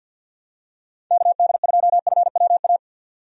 Morse code, heard as a keyed tone, about 51 wpm, 700 Hz. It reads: KD2FWA
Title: USA Call Signs - 6 Characters - 50wpm